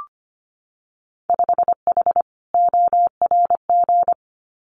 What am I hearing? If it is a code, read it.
Morse code, heard as a keyed tone, about 25 wpm, 700 Hz. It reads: 5H ORG
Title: Winter Field Day Exchanges Encore 25wpm